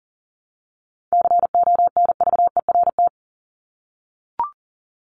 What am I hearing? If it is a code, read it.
Morse code, heard as a keyed tone, about 40 wpm, 700 Hz. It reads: CONVERT
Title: Top 1751-2000 Words 40wpm